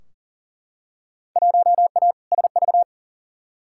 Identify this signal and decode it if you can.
Morse code, heard as a keyed tone, about 37 wpm, 700 Hz. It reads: 1A SV
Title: Field Day Exchanges Encore - 40wpm